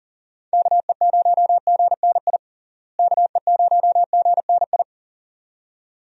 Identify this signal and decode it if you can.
Morse code, heard as a keyed tone, about 40 wpm, 700 Hz. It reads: KE0GNI KE0GNI
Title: USA Call Signs - 6 Characters - Repeated 2x - 40wpm